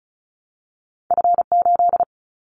Morse code, heard as a keyed tone, about 36 wpm, 690 Hz. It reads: F8
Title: Callsign Prefixes for Top 50 DXCC - 35wpm